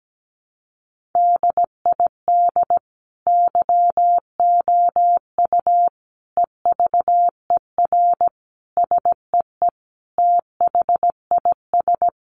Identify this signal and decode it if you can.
Morse code, heard as a keyed tone, about 17 wpm, 700 Hz. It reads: DID YOU EVER SEE THIS
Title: Sentences from Top 1500 Words 17wpm